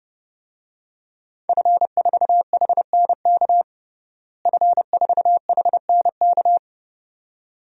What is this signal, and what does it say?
Morse code, heard as a keyed tone, about 32 wpm, 700 Hz. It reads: F4HNK F4HNK
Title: DXCC-100 Call Signs - 5 Characters - Repeated 2x - 30wpm